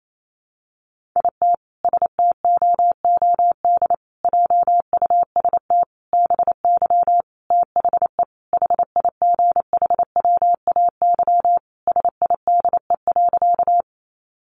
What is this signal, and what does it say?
Morse code, heard as a keyed tone, about 28 wpm, 700 Hz. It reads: IT STOOD JUST BY THE HIGHWAY SIDE.
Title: Sentences from Top 1400 Words 28wpm